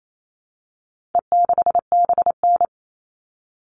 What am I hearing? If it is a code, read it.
Morse code, heard as a keyed tone, about 28 wpm, 700 Hz. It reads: E6BN
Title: All Cut Numbers - 4-Digit 28wpm